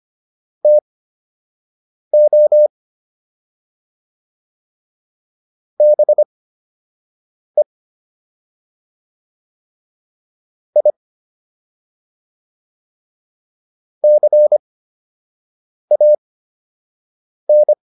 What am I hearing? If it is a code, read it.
Morse code, heard as a keyed tone, about 25 wpm, spaced as for 6 wpm, 600 Hz. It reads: TO BE I CAN